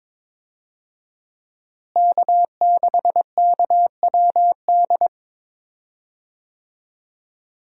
Morse code, heard as a keyed tone, about 22 wpm, 700 Hz. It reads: K6KWD
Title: USA Call Signs - 5 Characters - Repeated 3x - 22wpm